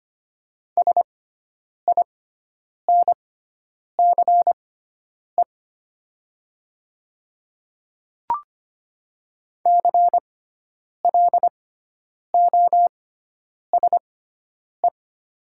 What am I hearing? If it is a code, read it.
Morse code, heard as a keyed tone, about 25 wpm, spaced as for 9 wpm, 700 Hz. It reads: SINCE CLOSE
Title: 6x Character Spacing - Top 300 Words - 25wpm